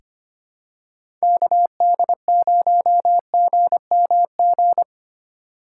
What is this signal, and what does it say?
Morse code, heard as a keyed tone, about 25 wpm, 700 Hz. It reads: KD0GMG
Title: USA Call Signs - 6 Characters - 2x Character Spacing - 25wpm